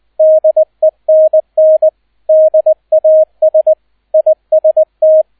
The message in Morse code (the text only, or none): DENN DAS IST